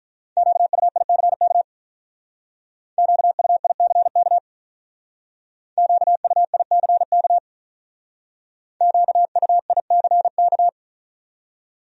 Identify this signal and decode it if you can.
Morse code, heard as a keyed tone, about 42 wpm, 700 Hz. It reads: QUICK QUICK QUICK QUICK
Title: Speed-Racing Top 500 Words 35wpm